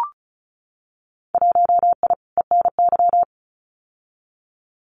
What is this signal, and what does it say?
Morse code, heard as a keyed tone, about 36 wpm, 700 Hz. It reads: 1I ENY
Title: Winter Field Day Exchanges Encore 35wpm